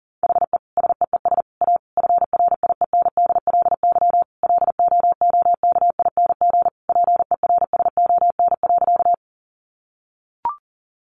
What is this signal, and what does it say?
Morse code, heard as a keyed tone, about 40 wpm, 700 Hz. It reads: HE SEES A FRIENDLY LOOKING PERSON.
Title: Easy Phrases 40wpm